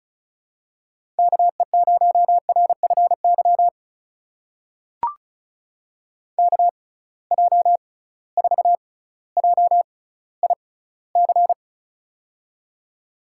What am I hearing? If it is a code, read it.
Morse code, heard as a keyed tone, about 36 wpm, spaced as for 18 wpm, 700 Hz. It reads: KE0RFY KJ4JIC
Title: USA Call Signs - 6 Characters - 6x Character Spacing - 35wpm